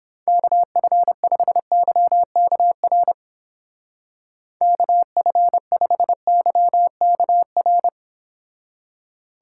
Morse code, heard as a keyed tone, about 31 wpm, 700 Hz. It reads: KF5YKR KF5YKR